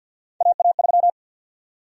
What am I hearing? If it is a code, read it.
Morse code, heard as a keyed tone, about 52 wpm, 700 Hz. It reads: AA3
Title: USA Call Sign Prefixes - 50wpm